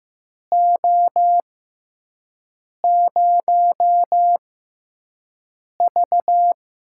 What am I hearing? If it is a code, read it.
Morse code, heard as a keyed tone, about 15 wpm, spaced as for 5 wpm, 700 Hz. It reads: O0V
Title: USA Call Signs - 4 Characters - 6x Character Spacing - 15wpm